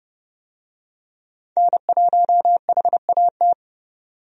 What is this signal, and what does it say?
Morse code, heard as a keyed tone, about 31 wpm, 700 Hz. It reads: N1HAT
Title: USA Call Signs - 5 Characters - Repeated 2x - 30wpm